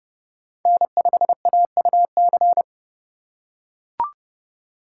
Morse code, heard as a keyed tone, about 29 wpm, 700 Hz. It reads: N5AUC